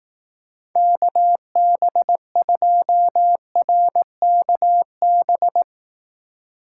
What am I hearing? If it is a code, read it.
Morse code, heard as a keyed tone, about 18 wpm, 700 Hz. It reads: KB2RKB
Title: Speed Racing - USA 2x3 Call Signs 15wpm